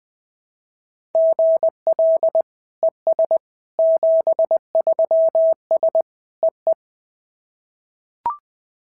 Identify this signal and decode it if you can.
Morse code, heard as a keyed tone, about 20 wpm, 690 Hz. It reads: GL ES 73S EE
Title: SOTA QSO Practice 20wpm